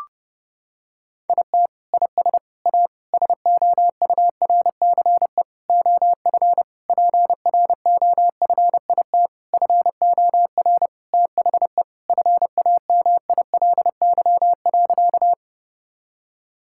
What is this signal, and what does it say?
Morse code, heard as a keyed tone, about 30 wpm, 700 Hz. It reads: IT IS A SOURCE OF PROFIT FOR THE FAMILY.
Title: Sentences from Top 1000 Words - Review Part 5 - 30wpm